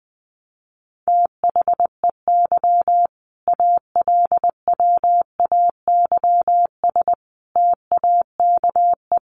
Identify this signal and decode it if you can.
Morse code, heard as a keyed tone, about 20 wpm, 700 Hz. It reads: THEY ALWAYS TAKE
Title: Sentences from Top 900 Words 20wpm